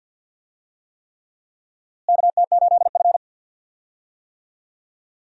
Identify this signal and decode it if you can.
Morse code, heard as a keyed tone, about 47 wpm, 700 Hz. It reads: KT8F
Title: USA Call Signs - 4 Characters - Repeated 3x - 50wpm